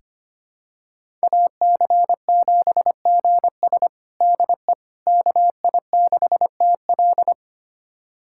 Morse code, heard as a keyed tone, about 25 wpm, 700 Hz. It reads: AC7GS DE KI6TL
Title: USA Callsign DE Callsign - 5 Characters - 25wpm